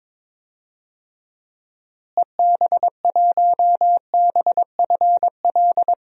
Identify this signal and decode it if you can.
Morse code, heard as a keyed tone, about 22 wpm, 700 Hz. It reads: EB1BFL
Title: DXCC-100 Call Signs - 6 Characters - Repeated 2x - 22wpm